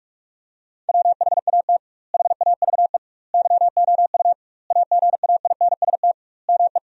Morse code, heard as a keyed tone, about 42 wpm, 700 Hz. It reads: WHAT HAVE YOU AGAINST ME
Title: 2x Word-Spacing - Sentences from Top 400 Words - 45wpm